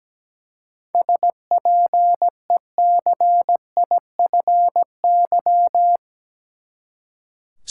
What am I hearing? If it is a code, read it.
Morse code, heard as a keyed tone, about 17 wpm, 700 Hz. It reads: SPECIFY
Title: Seven Letter Words - 17wpm